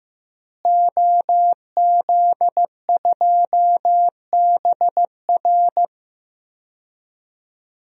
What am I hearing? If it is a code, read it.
Morse code, heard as a keyed tone, about 15 wpm, 700 Hz. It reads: OZ2BR
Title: DXCC-100 Call Signs - 5 Characters - Repeated 3x - 15wpm